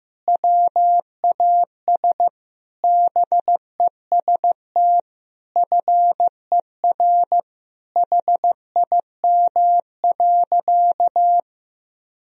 Morse code, heard as a keyed tone, about 15 wpm, 700 Hz. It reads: WAS BEST FER HIM.
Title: Sentences from Top 500 Words and Abbreviations 15wpm